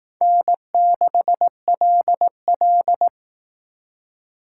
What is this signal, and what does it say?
Morse code, heard as a keyed tone, about 18 wpm, 700 Hz. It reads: N6LL
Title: Speed Racing - USA 1x2 Call Signs 15wpm